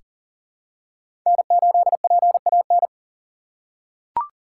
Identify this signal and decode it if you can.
Morse code, heard as a keyed tone, about 40 wpm, 700 Hz. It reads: N8PAN
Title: USA Call Signs - 5 Characters - 4x Character Spacing - 40wpm